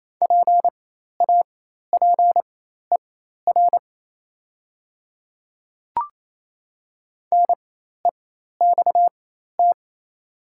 Morse code, heard as a keyed tone, about 28 wpm, spaced as for 13 wpm, 700 Hz. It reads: PAPER NEXT
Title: 4x Character Spacing - Top 300 Words - 28wpm